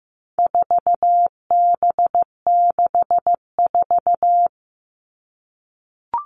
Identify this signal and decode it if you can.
Morse code, heard as a keyed tone, about 15 wpm, 700 Hz. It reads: 4B64
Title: All Cut Numbers - 4-Digit 15wpm